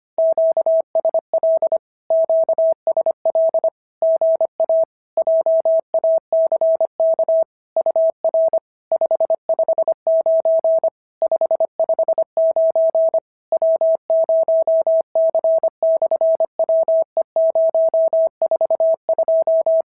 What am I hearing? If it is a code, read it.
Morse code, heard as a keyed tone, about 25 wpm, 650 Hz. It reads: QSL QSL GA JACK UR 559 559 W0C/WE042
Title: SOTA QSO Practice 25wpm